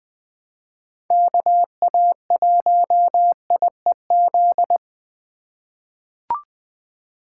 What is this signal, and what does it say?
Morse code, heard as a keyed tone, about 20 wpm, 700 Hz. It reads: KA1IEZ
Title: USA Call Signs - 6 Characters - 2x Character Spacing - 20wpm